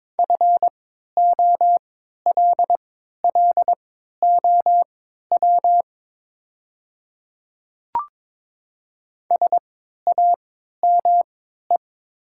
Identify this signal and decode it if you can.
Morse code, heard as a keyed tone, about 22 wpm, spaced as for 12 wpm, 700 Hz. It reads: FOLLOW SAME
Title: 3x Character Spacing - Top 200 Words - 22wpm